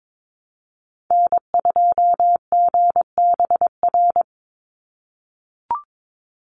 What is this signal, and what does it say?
Morse code, heard as a keyed tone, about 22 wpm, 700 Hz. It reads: N2GBR
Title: NA SOTA Chaser Callsigns 22wpm